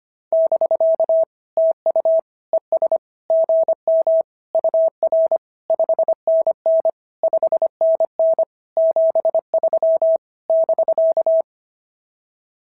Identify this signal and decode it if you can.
Morse code, heard as a keyed tone, about 25 wpm, 690 Hz. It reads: <BK> TU ES GM UR 5NN 5NN 73 <BK>
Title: SOTA QSO Practice 25wpm